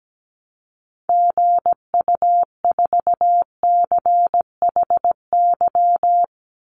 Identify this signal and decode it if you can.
Morse code, heard as a keyed tone, about 17 wpm, 700 Hz. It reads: GU4CHY
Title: Call Signs Repeated 3x - 17wpm